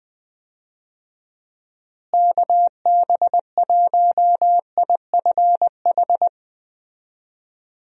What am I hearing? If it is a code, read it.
Morse code, heard as a keyed tone, about 20 wpm, 700 Hz. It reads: KB1IFH
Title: USA Call Signs - 6 Characters - Repeated 3x - 20wpm